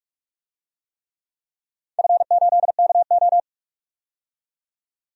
Morse code, heard as a keyed tone, about 42 wpm, 700 Hz. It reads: F8KO
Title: DXCC-100 Call Signs - 4 Characters - 45wpm